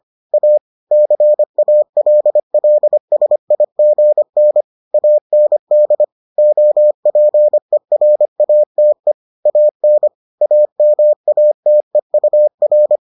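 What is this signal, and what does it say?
Morse code, heard as a keyed tone, about 25 wpm, 600 Hz. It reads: A CALLSIGN AND OPERATE AN AMATEUR
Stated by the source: Morse edition of the Foundations of Amateur Radio episodes.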